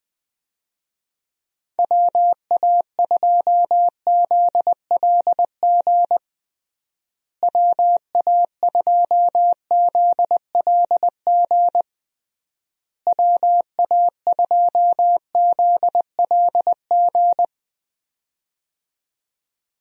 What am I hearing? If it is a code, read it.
Morse code, heard as a keyed tone, about 20 wpm, 700 Hz. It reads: WA2ZLG WA2ZLG WA2ZLG
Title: USA Call Signs - 6 Characters - Repeated 3x - 20wpm